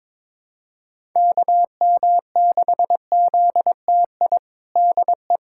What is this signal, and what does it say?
Morse code, heard as a keyed tone, about 22 wpm, 700 Hz. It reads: KM6ZTI DE
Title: USA Callsign DE Callsign - 6 Characters - 22wpm